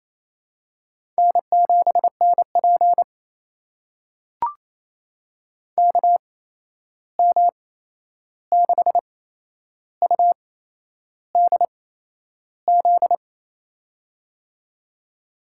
Morse code, heard as a keyed tone, about 28 wpm, spaced as for 11 wpm, 700 Hz. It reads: N7NP KM6UDZ
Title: USA Call Signs - 8x Character Spacing - 28wpm